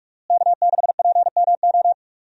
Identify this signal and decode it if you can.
Morse code, heard as a keyed tone, about 45 wpm, 700 Hz. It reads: K6PMO
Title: USA Call Signs - 5 Characters - Repeated 2x - 45wpm